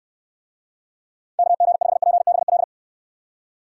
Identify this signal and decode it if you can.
Morse code, heard as a keyed tone, about 63 wpm, 700 Hz. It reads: KG5PBL